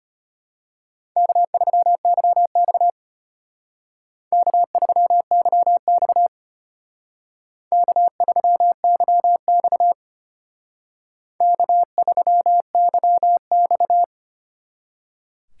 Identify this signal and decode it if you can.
Morse code, heard as a keyed tone, about 29 wpm, 700 Hz. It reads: K3YX K3YX K3YX K3YX
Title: Speed Racing - USA 1x2 Call Signs 25wpm